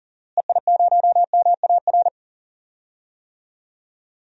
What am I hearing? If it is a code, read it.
Morse code, heard as a keyed tone, about 40 wpm, 700 Hz. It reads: EI0MAR
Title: DXCC-100 Call Signs - 6 Characters - Repeated 3x - 40wpm